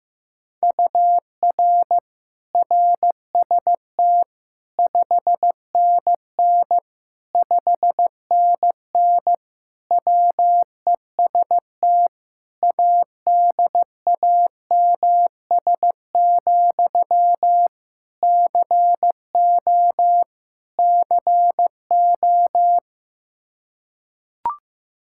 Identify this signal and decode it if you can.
Morse code, heard as a keyed tone, about 15 wpm, 700 Hz. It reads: UR RST 5NN 5NN WEST ADAMS, CO CO
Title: K1USN SST QSO Practice 15wpm